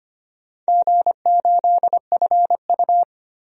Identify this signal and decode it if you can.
Morse code, heard as a keyed tone, about 25 wpm, 700 Hz. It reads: G8FU
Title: DXCC-100 Call Signs - 4 Characters - Repeated 2x - 25wpm